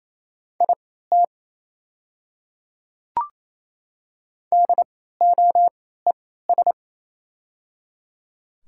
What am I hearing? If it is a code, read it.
Morse code, heard as a keyed tone, about 28 wpm, spaced as for 16 wpm, 700 Hz. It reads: IT DOES
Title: 3x Character Spacing - Top 300 Words - 28wpm